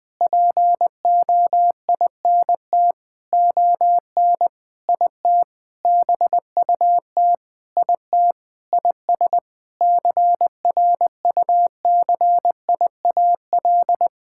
Morse code, heard as a keyed tone, about 20 wpm, 700 Hz. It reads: POINT ON IT BUT IT IS CRUCIAL
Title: Sentences with 9-Word Phrases - 20wpm